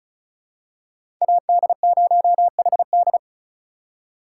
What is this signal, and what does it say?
Morse code, heard as a keyed tone, about 34 wpm, 700 Hz. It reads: AD0HD
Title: USA Call Signs - 5 Characters - 3x Character Spacing - 35wpm